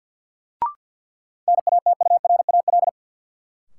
Morse code, heard as a keyed tone, about 47 wpm, 700 Hz. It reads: NATURAL
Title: Sentences from Top 700 Words - New Word Drill - 50wpm